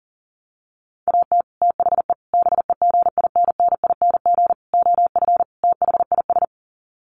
Morse code, heard as a keyed tone, about 40 wpm, 700 Hz. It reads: AT THE BEGINNING OF THIS
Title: Sets of 5 Words Encore 40wpm